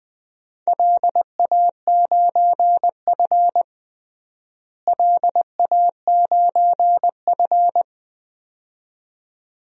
Morse code, heard as a keyed tone, about 20 wpm, 700 Hz. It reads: LA9F LA9F
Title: DXCC-100 Call Signs - 4 Characters - Repeated 2x - 20wpm